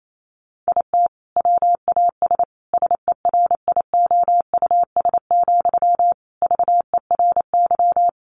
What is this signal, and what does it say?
Morse code, heard as a keyed tone, about 28 wpm, 700 Hz. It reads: IT WAS SERIOUS, VERY
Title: Sentences from Top 1100 Words 28wpm